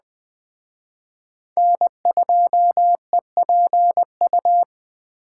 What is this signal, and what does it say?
Morse code, heard as a keyed tone, about 20 wpm, 700 Hz. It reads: N2EPU
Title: Speed Racing - USA 1x3 Call Signs 17wpm